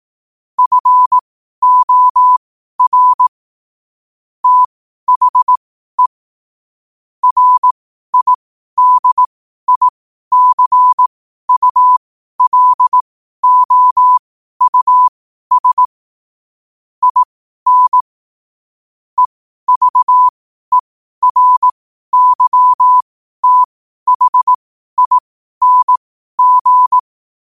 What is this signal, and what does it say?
Morse code, heard as a keyed tone, about 18 wpm, spaced as for 12 wpm, 1000 Hz. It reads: FOR THE RIDICULOUS IN EVERYTHING
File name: qotdfiles / 12wpm